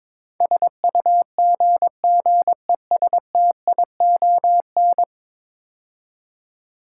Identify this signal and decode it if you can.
Morse code, heard as a keyed tone, about 22 wpm, 700 Hz. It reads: SUGGESTION
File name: practice / 22wpm / 10 Letter Words - First 3 Letters Given - 22wpm